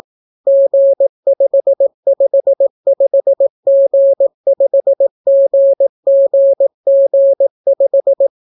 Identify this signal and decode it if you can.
Morse code, heard as a keyed tone, about 18 wpm, 550 Hz. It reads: G555G5GGG5